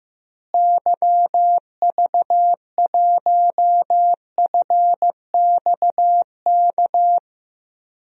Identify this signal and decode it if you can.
Morse code, heard as a keyed tone, about 15 wpm, 700 Hz. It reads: YV1FXK
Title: DXCC-100 Call Signs - 6 Characters - Repeated 3x - 15wpm